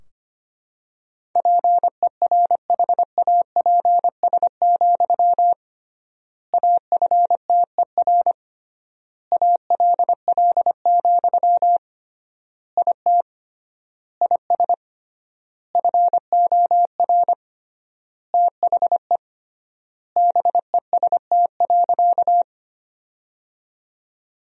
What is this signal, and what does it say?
Morse code, heard as a keyed tone, about 25 wpm, 700 Hz. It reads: PERHAPS, AFTER ALL, IT IS FOR THE BEST.
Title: 3x Word-Spacing - Sentences from Top 500 Words - 25wpm